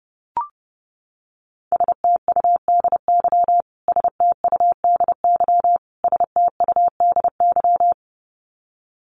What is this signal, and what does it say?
Morse code, heard as a keyed tone, about 30 wpm, 700 Hz. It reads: STUDY STUDY STUDY
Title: Top 300 Words Repeated 30wpm